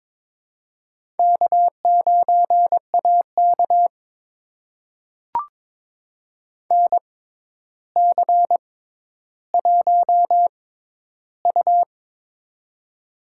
Morse code, heard as a keyed tone, about 22 wpm, spaced as for 11 wpm, 700 Hz. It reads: K9AK NC1U
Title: USA Call Signs - 4 Characters - 6x Character Spacing - 22wpm